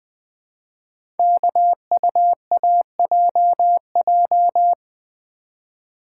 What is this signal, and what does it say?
Morse code, heard as a keyed tone, about 21 wpm, 700 Hz. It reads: KUAJJ
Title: Code Groups - 5 Letters - 20wpm